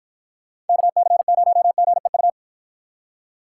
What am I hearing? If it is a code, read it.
Morse code, heard as a keyed tone, about 52 wpm, 700 Hz. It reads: KC0GEU